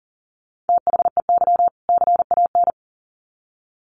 Morse code, heard as a keyed tone, about 40 wpm, 700 Hz. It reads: THEY CAN
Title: Sets of 2 Words 40wpm